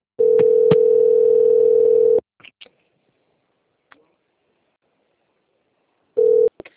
Ringback tone (North America), 440 + 480 Hz.
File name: out- - - - - .21577